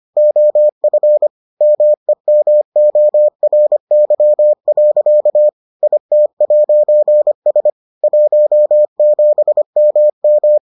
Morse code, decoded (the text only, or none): OF MEMORY. IT'S 17MM